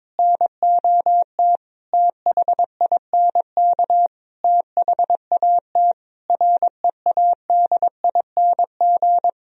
Morse code, heard as a keyed tone, about 22 wpm, 700 Hz. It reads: NOT THINK THAT READING
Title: Sentences from Top 1000 Words - Review Part 3 - 22wpm